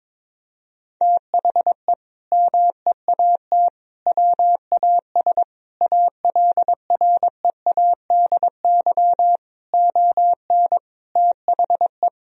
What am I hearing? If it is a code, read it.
Morse code, heard as a keyed tone, about 21 wpm, 700 Hz. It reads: THE MEAT WAS ALREADY ON THE
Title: Sentences from Top 1000 Words - Review Part 3 - 22wpm